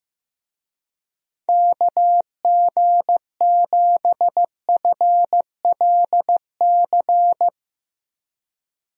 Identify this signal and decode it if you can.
Morse code, heard as a keyed tone, about 15 wpm, 700 Hz. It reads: KG7FLC
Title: USA Call Signs - 6 Characters - Repeated 2x - 15wpm